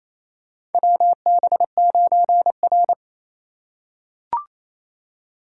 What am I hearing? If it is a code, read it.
Morse code, heard as a keyed tone, about 28 wpm, 700 Hz. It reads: WB9R
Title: USA Call Signs - 4 Characters - 4x Character Spacing - 28wpm